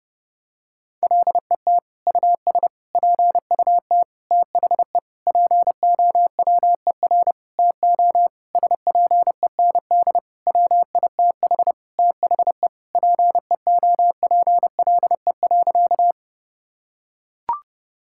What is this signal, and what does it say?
Morse code, heard as a keyed tone, about 30 wpm, 700 Hz. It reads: LET US PUT THE POWER TO SPEND WITH THE PEOPLE.
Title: Sentences from Top 1000 Words - Review Part 5 - 30wpm